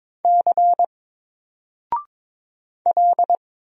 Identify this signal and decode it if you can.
Morse code, heard as a keyed tone, about 22 wpm, 700 Hz. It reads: C L
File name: Single-Letter-Number-22wpm